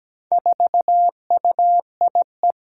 Morse code, heard as a keyed tone, about 17 wpm, 700 Hz. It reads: 4UIE
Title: Four Challenging Letter-Number Combinations 17wpm